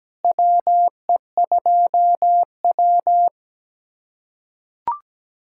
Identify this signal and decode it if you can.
Morse code, heard as a keyed tone, about 17 wpm, 700 Hz. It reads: WE2W
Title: USA Callsign DE Callsign 17wpm